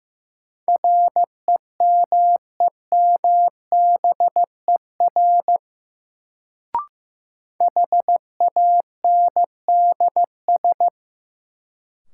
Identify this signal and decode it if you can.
Morse code, heard as a keyed tone, about 15 wpm, 700 Hz. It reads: REMEMBER HANDS
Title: Sets of 4 Words - New Word Drill - 15wpm